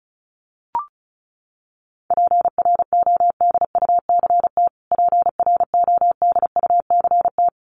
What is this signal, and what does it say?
Morse code, heard as a keyed tone, about 34 wpm, 700 Hz. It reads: PRODUCT PRODUCT
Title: Top 500 Words Repeated 35wpm